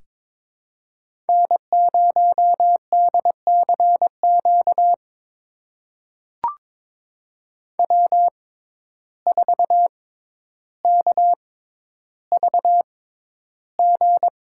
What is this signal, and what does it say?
Morse code, heard as a keyed tone, about 22 wpm, spaced as for 11 wpm, 700 Hz. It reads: N0DCQ W4KVG
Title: USA Call Signs - 5 Characters - 6x Character Spacing - 22wpm